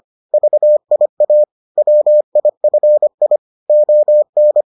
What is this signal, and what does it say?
Morse code, heard as a keyed tone, about 25 wpm, 600 Hz. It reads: VIA WIFI ON